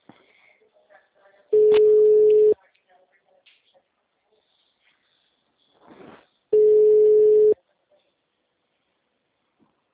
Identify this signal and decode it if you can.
Ringback tone (Europe, 425 Hz), with speech.
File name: out- -2017- - - .2755